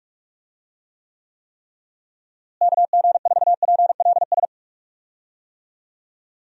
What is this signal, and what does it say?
Morse code, heard as a keyed tone, about 46 wpm, 700 Hz. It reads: KG4PLS
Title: USA Call Signs - 6 Characters - Repeated 3x - 45wpm